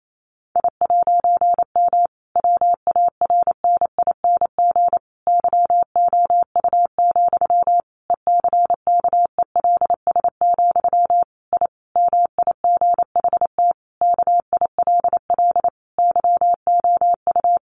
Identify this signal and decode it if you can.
Morse code, heard as a keyed tone, about 28 wpm, 700 Hz. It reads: I'M WARNING YOU, ECKELS, I MIGHT KILL YOU
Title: A Sound Of Thunder 28wpm